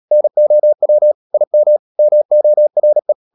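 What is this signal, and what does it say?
Morse code, heard as a keyed tone, about 37 wpm, 600 Hz. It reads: NOW IM MORE
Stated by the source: Morse code practice at 35–39 wpm sending English text of Ken Follett's novel Never, from CW med Gnister.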